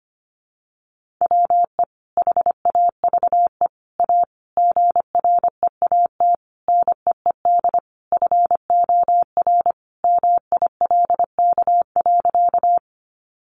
Morse code, heard as a keyed tone, about 25 wpm, 700 Hz. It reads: WE HAVE A GREAT NEED FOR MILK.
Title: Sentences from Top 1500 Words 25wpm